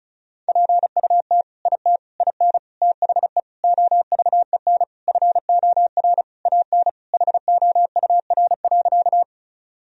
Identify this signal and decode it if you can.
Morse code, heard as a keyed tone, about 36 wpm, 700 Hz. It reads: PUT IT IN THE OVEN FOR AN HOUR.
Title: Sentences from Top 1000 Words - Review Part 4 - 35wpm